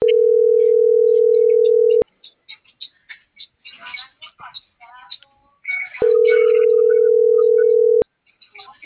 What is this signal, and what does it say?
Ringback tone (North America), 440 + 480 Hz.
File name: CENTRAL BLN / out hector.tellez